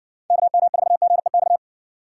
Morse code, heard as a keyed tone, about 58 wpm, 700 Hz. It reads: KM4GEX